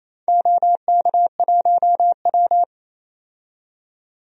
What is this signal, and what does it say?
Morse code, heard as a keyed tone, about 28 wpm, 700 Hz. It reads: OK1W
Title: DXCC-100 Call Signs - 4 Characters - 28wpm